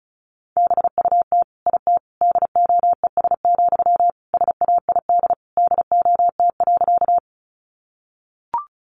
Morse code, heard as a keyed tone, about 34 wpm, 700 Hz. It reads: BUT IT DOES, SAID DOT.
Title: Sentences from Top 1500 Words 35wpm